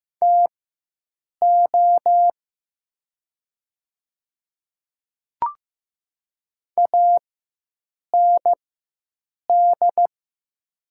Morse code, heard as a keyed tone, about 15 wpm, spaced as for 7 wpm, 700 Hz. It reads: TO AND
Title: 4x Character Spacing - Top 100 Words - 15wpm